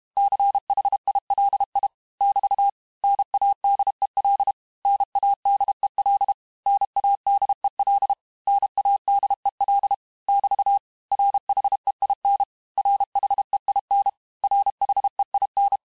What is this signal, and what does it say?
Morse code, heard as a keyed tone, about 32 wpm, 800 Hz. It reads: CHILI <BT> NADEL NADEL NADEL NADEL <BT> RHEIN RHEIN RHEIN
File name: cours-du-mercredi / audios / cours-868-teste-MOTS-vitesse-30mpm